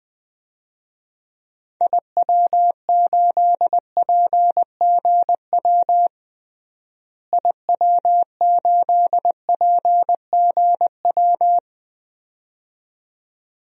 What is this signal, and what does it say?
Morse code, heard as a keyed tone, about 20 wpm, 700 Hz. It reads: IW8PGW IW8PGW
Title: DXCC-100 Call Signs - 6 Characters - Repeated 2x - 20wpm